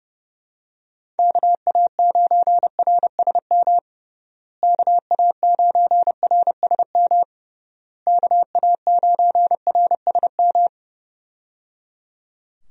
Morse code, heard as a keyed tone, about 30 wpm, 700 Hz. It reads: KA9RSM KA9RSM KA9RSM
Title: USA Call Signs - 6 Characters - Repeated 3x - 30wpm